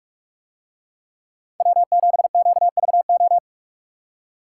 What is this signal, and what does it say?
Morse code, heard as a keyed tone, about 45 wpm, 700 Hz. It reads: W7QVO